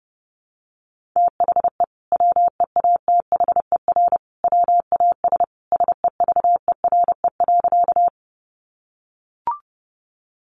Morse code, heard as a keyed tone, about 30 wpm, 700 Hz. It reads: THE WEATHER WAS SEVERE.
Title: Sentences from Top 1400 Words 30wpm